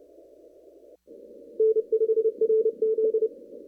Morse code, heard as a keyed tone, about 27 wpm, 440 Hz. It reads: N5RZ
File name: CW / kt5j wpx cw-op-k5pi